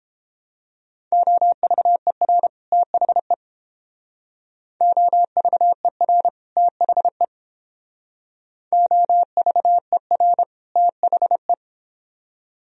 Morse code, heard as a keyed tone, about 31 wpm, 700 Hz. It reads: OVER THE OVER THE OVER THE